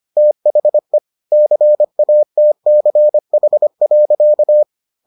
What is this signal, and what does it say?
Morse code, heard as a keyed tone, about 25 wpm, 600 Hz. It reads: THE CATCH.